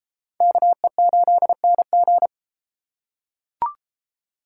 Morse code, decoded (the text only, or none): KE8NG